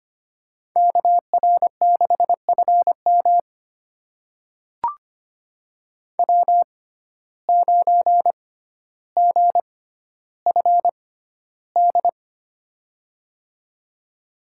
Morse code, heard as a keyed tone, about 25 wpm, spaced as for 13 wpm, 700 Hz. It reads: KR6FM W9GFD
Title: USA Call Signs - 5 Characters - 6x Character Spacing - 25wpm